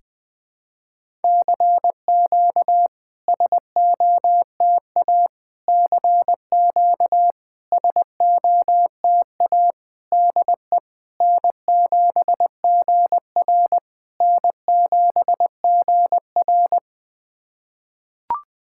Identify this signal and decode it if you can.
Morse code, heard as a keyed tone, about 20 wpm, 700 Hz. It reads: CQ SOTA CQ SOTA DE N7GR N7GR
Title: SOTA QSO Practice 20wpm